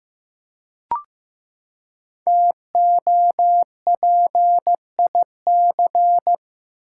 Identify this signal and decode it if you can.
Morse code, heard as a keyed tone, about 15 wpm, 700 Hz. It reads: TOPIC
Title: Five Letter Words - 15wpm